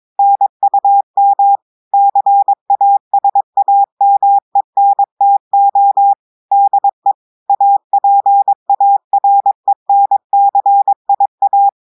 Morse code, heard as a keyed tone, about 22 wpm, 800 Hz. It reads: NUM CASAMENTO DE APARENCIA
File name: audios / portugais / portugais texte en clair 1 vitesse 22 mots minutes